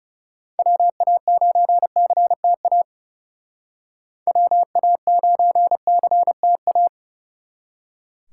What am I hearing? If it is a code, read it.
Morse code, heard as a keyed tone, about 35 wpm, 700 Hz. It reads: WA9CTA WA9CTA
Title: Speed Racing - USA 2x3 Call Signs 30wpm